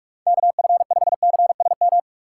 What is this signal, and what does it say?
Morse code, heard as a keyed tone, about 43 wpm, 700 Hz. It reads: KF5CSM